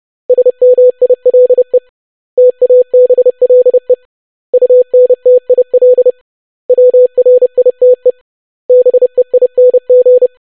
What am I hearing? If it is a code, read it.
Morse code, heard as a keyed tone, about 30 wpm, 500 Hz. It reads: SMILE TABLE UNTIL WRITE BEING
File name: basic500-25wpm